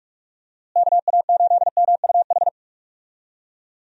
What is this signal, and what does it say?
Morse code, heard as a keyed tone, about 46 wpm, 700 Hz. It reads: KA8MUH
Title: USA Call Signs - 6 Characters - Repeated 3x - 45wpm